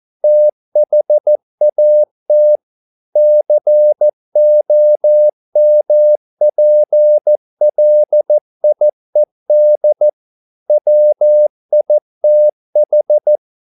Morse code, heard as a keyed tone, about 14 wpm, 600 Hz. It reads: THAT COMPLIED WITH